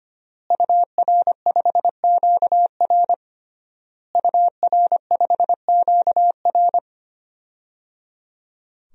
Morse code, heard as a keyed tone, about 25 wpm, 700 Hz. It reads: UR5QR UR5QR
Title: DXCC-100 Call Signs - 5 Characters - Repeated 3x - 25wpm